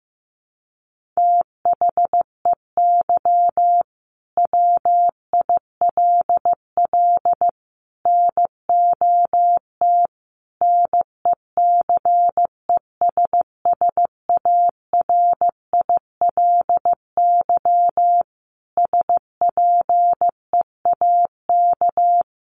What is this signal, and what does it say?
Morse code, heard as a keyed tone, about 15 wpm, 700 Hz. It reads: THEY WILL NOT NECESSARILY SPEAK
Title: Sentences from Top 1400 Words 15wpm